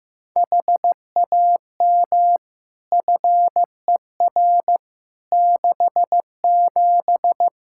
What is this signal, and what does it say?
Morse code, heard as a keyed tone, about 15 wpm, 700 Hz. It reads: HAM FER 67
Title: Practice QSO - Exchange 3 - More Info - 15wpm